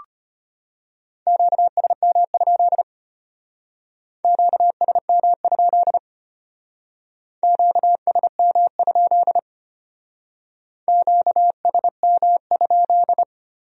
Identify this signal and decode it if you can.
Morse code, heard as a keyed tone, about 32 wpm, 700 Hz. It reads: QSM? QSM? QSM? QSM?